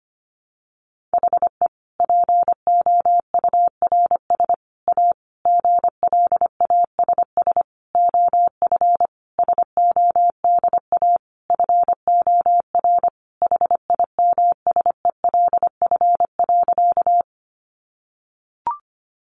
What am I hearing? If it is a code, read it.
Morse code, heard as a keyed tone, about 25 wpm, 700 Hz. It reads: HE POURS A GLASS OF SODA FOR HIMSELF.
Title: Additional Easy Phrases 25wpm